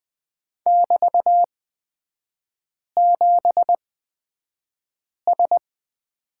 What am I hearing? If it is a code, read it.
Morse code, heard as a keyed tone, about 20 wpm, spaced as for 5 wpm, 700 Hz. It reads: <BT>7S
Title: 243 Lesson 39e - Letter z - Three Characters 20wpm